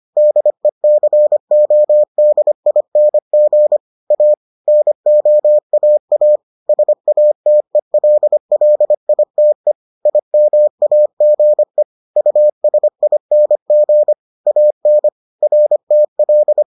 Morse code, heard as a keyed tone, about 25 wpm, 600 Hz. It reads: DECODING A NOAA SATELLITE IMAGE USING AN RTL